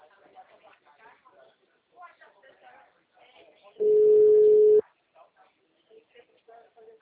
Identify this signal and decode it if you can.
Ringback tone (Europe, 425 Hz), with speech.
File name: out- -2016- - - .48605